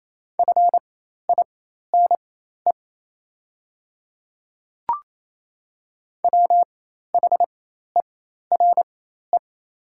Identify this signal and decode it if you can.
Morse code, heard as a keyed tone, about 27 wpm, spaced as for 13 wpm, 690 Hz. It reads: FINE WHERE
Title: 4x Character Spacing - Top 500 Words - 28wpm